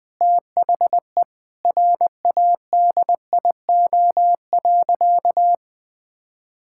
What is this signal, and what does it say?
Morse code, heard as a keyed tone, about 20 wpm, 700 Hz. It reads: THE RADIO.
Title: Sentences with 5-Word Phrases - Spoken Beginning - 20wpm